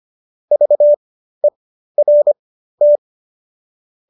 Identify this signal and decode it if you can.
Morse code, heard as a keyed tone, about 25 wpm, spaced as for 13 wpm, 590 Hz. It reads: VERT